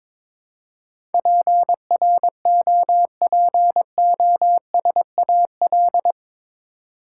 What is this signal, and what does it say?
Morse code, heard as a keyed tone, about 22 wpm, 700 Hz. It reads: PROPOSAL
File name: practice / 22wpm / Eight Letter Words - 22wpm